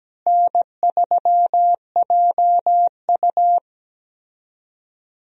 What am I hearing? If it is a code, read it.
Morse code, heard as a keyed tone, about 17 wpm, 700 Hz. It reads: N3JU
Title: USA Call Signs - 4 Characters - Repeated 3x - 17wpm